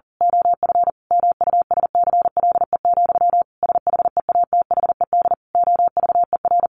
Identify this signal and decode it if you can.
Morse code, heard as a keyed tone, about 40 wpm, 700 Hz. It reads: OF MUSCLE, SHEATHED OVER
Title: A Sound Of Thunder 40wpm